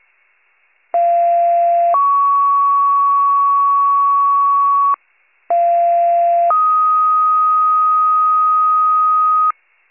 Two-tone page, heard as a one-second tone, then a three-second tone, about 690 then 1120 Hz, about 690 then 1250 Hz. It is heard from 0.9 to 9.5 s.